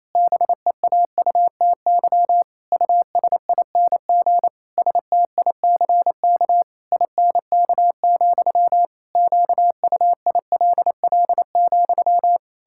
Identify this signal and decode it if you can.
Morse code, heard as a keyed tone, about 28 wpm, 700 Hz. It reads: BEAUTY USING STICK INK, QUILL,
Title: Pangram Challenge v2 - 28wpm